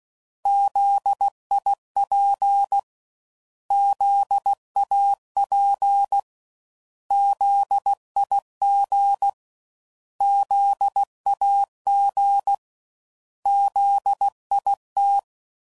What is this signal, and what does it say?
Morse code, heard as a keyed tone, about 16 wpm, 800 Hz. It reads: ZIP ZAP ZIG ZAG ZIT
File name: Prefixes / By First Letter